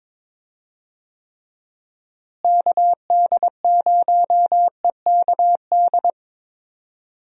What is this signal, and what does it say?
Morse code, heard as a keyed tone, about 22 wpm, 700 Hz. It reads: KD0EKD
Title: USA Call Signs - 6 Characters - Repeated 3x - 22wpm